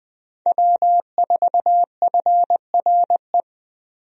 Morse code, heard as a keyed tone, about 20 wpm, 700 Hz. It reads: W4FRE
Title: USA Call Signs - 5 Characters - Repeated 3x - 20wpm